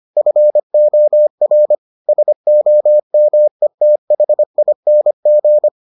Morse code, heard as a keyed tone, about 25 wpm, 600 Hz. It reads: FOR SOMETHING